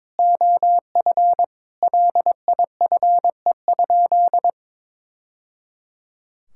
Morse code, heard as a keyed tone, about 22 wpm, 700 Hz. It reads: OF LIFE?
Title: Sentences from Top 1000 Words - Review Part 5 - 22wpm